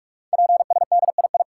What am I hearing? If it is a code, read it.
Morse code, heard as a keyed tone, about 44 wpm, 690 Hz. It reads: PSDII
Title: Code Groups - 5 Letter:Numbers - 45wpm